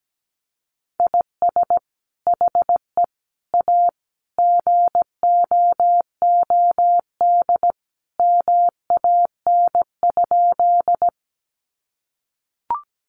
Morse code, heard as a keyed tone, about 17 wpm, 700 Hz. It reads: IS HE A GOOD MAN?
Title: Sentences from Top 200 Words 17wpm